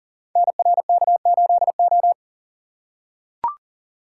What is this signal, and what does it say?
Morse code, heard as a keyed tone, about 39 wpm, 690 Hz. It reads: NRK8O
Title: Code Groups - 5 Letter:Numbers - 40wpm